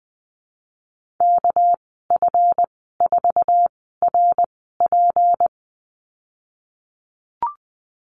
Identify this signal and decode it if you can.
Morse code, heard as a keyed tone, about 20 wpm, spaced as for 15 wpm, 700 Hz. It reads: KF4RP
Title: 200 Lesson 32g - Letter k - Call Signs 20wpm